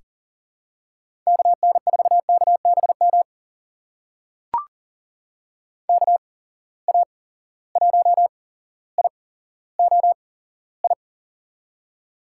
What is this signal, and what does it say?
Morse code, heard as a keyed tone, about 37 wpm, spaced as for 17 wpm, 700 Hz. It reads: KN4KBM KA1IOI
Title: USA Call Signs - 6 Characters - 8x Character Spacing - 40wpm